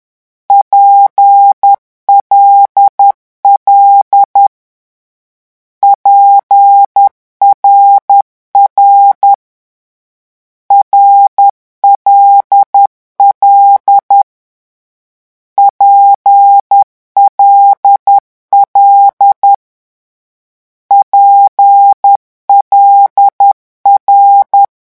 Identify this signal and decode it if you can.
Morse code, heard as a keyed tone, about 10 wpm, 800 Hz. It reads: PLL PRR RLL PLL PLR
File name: morse mp3 / lecon6-vitesse 10